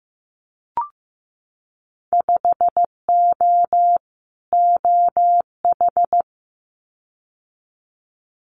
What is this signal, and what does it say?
Morse code, heard as a keyed tone, about 15 wpm, 700 Hz. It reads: 5O OH
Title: Winter Field Day Exchanges Encore 15wpm